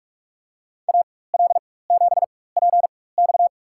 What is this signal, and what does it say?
Morse code, heard as a keyed tone, about 46 wpm, spaced as for 21 wpm, 700 Hz. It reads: AL7PX
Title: USA Call Signs - 5 Characters - 4x Character Spacing - 45wpm